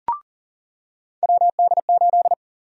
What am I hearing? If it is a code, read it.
Morse code, heard as a keyed tone, about 40 wpm, 700 Hz. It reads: WD8
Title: USA Call Sign Prefixes - 40wpm